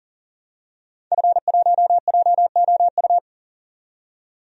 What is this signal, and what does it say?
Morse code, heard as a keyed tone, about 40 wpm, 700 Hz. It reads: F1JOU
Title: DXCC-100 Call Signs - 5 Characters - 40wpm